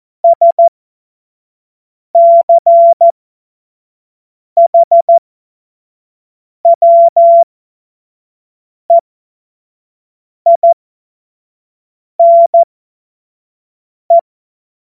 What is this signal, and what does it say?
Morse code, heard as a keyed tone, about 14 wpm, spaced as for 5 wpm, 680 Hz. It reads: SCHWEINE
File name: Morsekurs / Witze / Witz 30. August 2022 25 F70